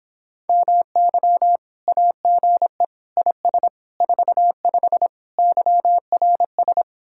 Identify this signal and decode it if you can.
Morse code, heard as a keyed tone, about 26 wpm, 700 Hz. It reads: MY AGE IS 45 YRS